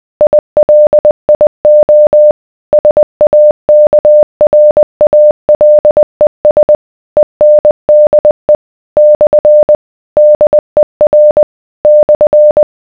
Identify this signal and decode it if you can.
Morse code, heard as a keyed tone, about 20 wpm, 600 Hz. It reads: ILIO SAKRALES ENDE / DER /